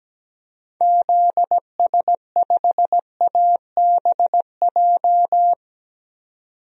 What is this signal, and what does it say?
Morse code, heard as a keyed tone, about 17 wpm, 700 Hz. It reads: ZS5ABJ
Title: DXCC-100 Call Signs - 6 Characters - 17wpm